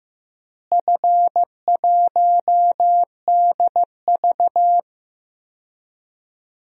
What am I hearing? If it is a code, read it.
Morse code, heard as a keyed tone, about 15 wpm, 700 Hz. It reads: F1DV
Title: DXCC-100 Call Signs - 4 Characters - Repeated 2x - 15wpm